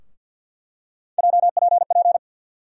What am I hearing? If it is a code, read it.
Morse code, heard as a keyed tone, about 47 wpm, 700 Hz. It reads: JPP
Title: USA Call Sign Postfixes - 3 Letters 50wpm